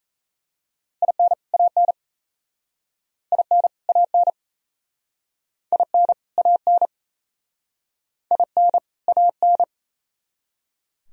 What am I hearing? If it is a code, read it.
Morse code, heard as a keyed tone, about 38 wpm, 700 Hz. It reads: IN AN IN AN IN AN IN AN